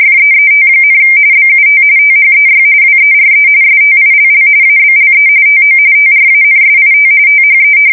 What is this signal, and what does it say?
RTTY, about 45.45 baud. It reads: FEET, BUT THEY WERE STILL ABLE TO TRANSIT THE
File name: fsk170 / battleships